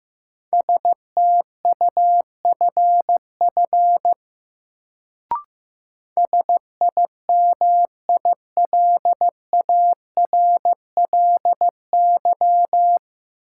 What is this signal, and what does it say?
Morse code, heard as a keyed tone, about 15 wpm, 700 Hz. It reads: STUFF SIMILARLY
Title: Sentences from Top 1400 Words - New Word Drill - 15wpm